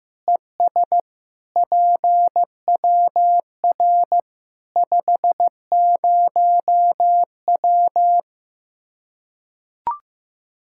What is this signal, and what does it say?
Morse code, heard as a keyed tone, about 15 wpm, 700 Hz. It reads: ES PWR 50W
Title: Practice QSO - Exchange 2 - Info - 15wpm